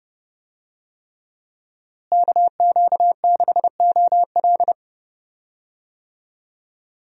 Morse code, heard as a keyed tone, about 30 wpm, 700 Hz. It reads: KQ6OL
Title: USA Call Signs - 5 Characters - Repeated 3x - 30wpm